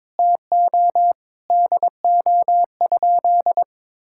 Morse code, heard as a keyed tone, about 22 wpm, 700 Hz. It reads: TO DO?
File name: practice / 22wpm / Sentences with 3-Word Phrases - 22wpm